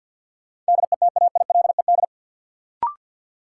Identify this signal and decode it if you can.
Morse code, heard as a keyed tone, about 48 wpm, 700 Hz. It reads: DETAILED
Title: Sentences from Top 1500 Words - New Word Drill Alt - 50wpm